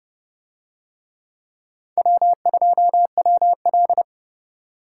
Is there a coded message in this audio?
Yes: Morse code from 2.0 to 4.0 s, heard as a keyed tone, about 30 wpm, 700 Hz. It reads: W2WL